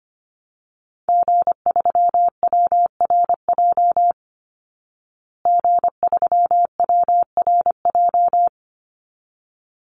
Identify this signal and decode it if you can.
Morse code, heard as a keyed tone, about 25 wpm, 700 Hz. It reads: G3WRJ G3WRJ
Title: Call Signs Repeated 3x - 25wpm